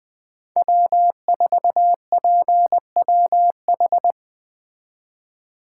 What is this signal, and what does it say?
Morse code, heard as a keyed tone, about 20 wpm, 700 Hz. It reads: W4PWH
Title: Speed Racing - USA 1x3 Call Signs 17wpm